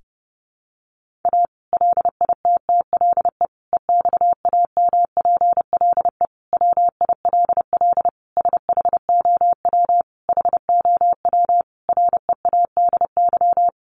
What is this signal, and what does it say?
Morse code, heard as a keyed tone, about 30 wpm, 700 Hz. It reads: A LITTLE EXAMPLE WILL SHOW HOW READY
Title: Sentences from Top 1500 Words 30wpm